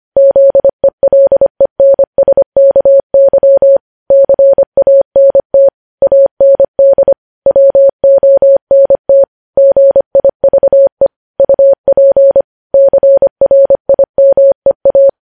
Morse code, heard as a keyed tone, about 25 wpm, 550 Hz. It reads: ZELENSKY CANT AND WONT GIVE UP CRIMEA
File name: podcast / News headlines-1x-25x25